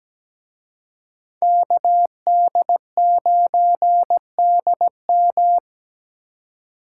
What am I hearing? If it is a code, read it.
Morse code, heard as a keyed tone, about 21 wpm, 700 Hz. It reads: KD9DM
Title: Speed Racing - USA 2x2 Call Signs 17wpm